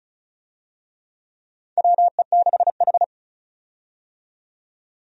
Morse code, heard as a keyed tone, about 34 wpm, 700 Hz. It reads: WE6H
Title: USA Call Signs - 4 Characters - 35wpm